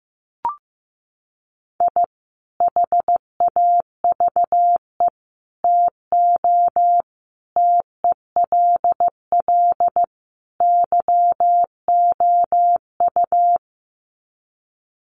Morse code, heard as a keyed tone, about 15 wpm, 700 Hz. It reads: I HAVE TO TELL YOU
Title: Sets of 5 Words 15wpm